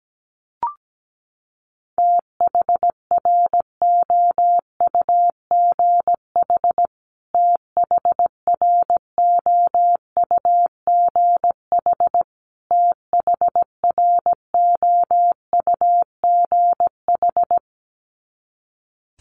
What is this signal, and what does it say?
Morse code, heard as a keyed tone, about 17 wpm, 700 Hz. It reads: THROUGH THROUGH THROUGH
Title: Top 200 Words Repeated 17wpm